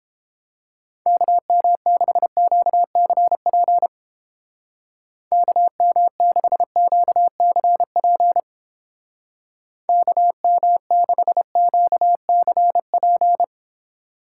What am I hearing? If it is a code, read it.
Morse code, heard as a keyed tone, about 29 wpm, 700 Hz. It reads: KM6QCP KM6QCP KM6QCP